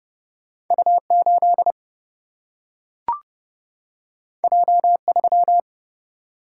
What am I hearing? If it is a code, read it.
Morse code, heard as a keyed tone, about 30 wpm, 700 Hz. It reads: U8 J3
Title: Single-Letter Single-Number 30wpm